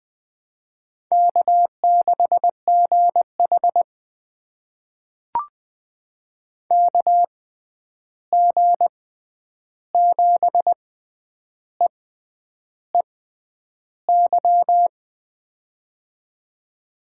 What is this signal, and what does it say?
Morse code, heard as a keyed tone, about 20 wpm, spaced as for 9 wpm, 700 Hz. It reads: K6GH KG7EEY
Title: USA Call Signs - 6x Character Spacing - 20wpm